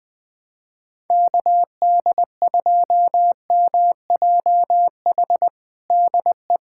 Morse code, heard as a keyed tone, about 19 wpm, 700 Hz. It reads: KD2MJH DE
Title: USA Callsign DE Callsign 20wpm